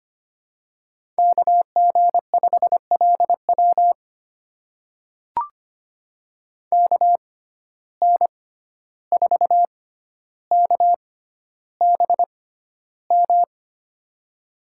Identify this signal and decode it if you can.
Morse code, heard as a keyed tone, about 25 wpm, spaced as for 12 wpm, 700 Hz. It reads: KG5LW KN4KBM
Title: USA Call Signs - 6x Character Spacing - 25wpm